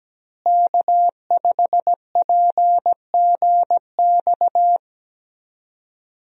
Morse code, heard as a keyed tone, about 17 wpm, 700 Hz. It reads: K5PGX
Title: USA Call Signs - 5 Characters - Repeated 2x - 17wpm